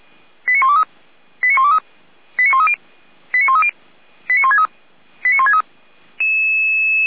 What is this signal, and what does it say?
Five-tone selective call (ZVEI), heard as quick runs of five tones, tones of about 70 ms. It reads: 89123 89123 89130 89130 89163 89163
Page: /455/SDR/ZVEI/